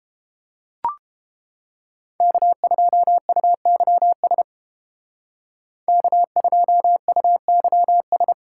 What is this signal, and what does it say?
Morse code, heard as a keyed tone, about 30 wpm, 700 Hz. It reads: K2UYS K2UYS